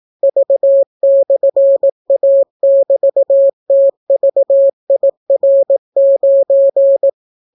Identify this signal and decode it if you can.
Morse code, heard as a keyed tone, about 18 wpm, 550 Hz. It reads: V/A<BT>TVIR9